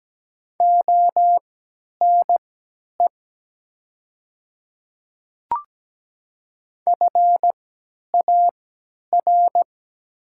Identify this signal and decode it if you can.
Morse code, heard as a keyed tone, about 17 wpm, spaced as for 10 wpm, 700 Hz. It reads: ONE FAR
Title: 3x Character Spacing - Top 300 Words - 17wpm